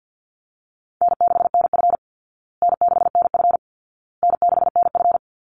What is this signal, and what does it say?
Morse code, heard as a keyed tone, about 50 wpm, 700 Hz. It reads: N6NF N6NF N6NF
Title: Easy Call Signs Repeated 3x - 50wpm